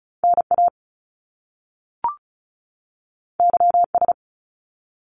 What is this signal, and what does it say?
Morse code, heard as a keyed tone, about 36 wpm, 700 Hz. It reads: NA YS
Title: Top Two Letter Combinations in Words 35wpm